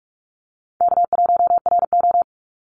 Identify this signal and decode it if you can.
Morse code, heard as a keyed tone, about 44 wpm, 700 Hz. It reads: K1RO
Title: Easy Call Signs Repeated 3x - 45wpm